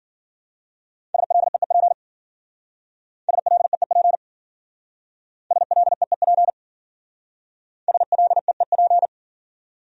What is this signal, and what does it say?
Morse code, heard as a keyed tone, about 52 wpm, 700 Hz. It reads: SLEEP SLEEP SLEEP SLEEP